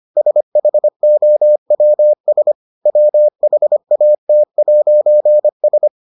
Morse code, heard as a keyed tone, about 25 wpm, 600 Hz. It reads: SHOWS WHAT'S